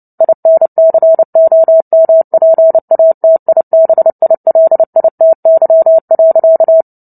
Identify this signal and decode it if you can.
Morse code, heard as a keyed tone, about 30 wpm, 650 Hz. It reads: INCOMPATIBILITY.
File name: Book Two / 03-Chapter 3 - 29 WPM